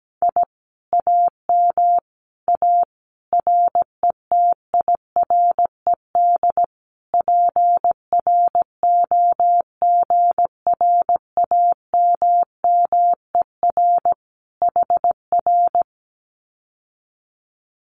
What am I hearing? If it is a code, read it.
Morse code, heard as a keyed tone, about 17 wpm, 700 Hz. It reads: I AM A RETIRED PROGRAMMER HR
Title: Common QSO Phrases Encore 17wpm